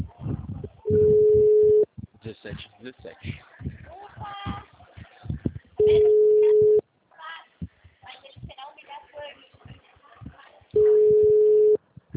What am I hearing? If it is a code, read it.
Ringback tone (Europe, 425 Hz), with speech.
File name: out- -2015- - - .69616